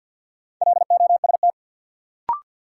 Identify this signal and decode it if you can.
Morse code, heard as a keyed tone, about 52 wpm, 700 Hz. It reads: LOST
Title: Sentences from Top 1400 Words - New Word Drill - 50wpm